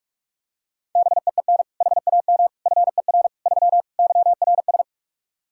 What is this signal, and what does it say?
Morse code, heard as a keyed tone, about 46 wpm, 700 Hz. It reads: BEEN HAM FER 3 YRS
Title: Practice QSO - Exchange 3 - More Info - 45wpm